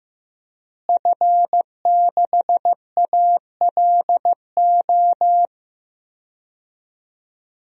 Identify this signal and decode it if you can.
Morse code, heard as a keyed tone, about 15 wpm, 700 Hz. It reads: F6ALO
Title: DXCC-100 Call Signs - 5 Characters - Repeated 3x - 15wpm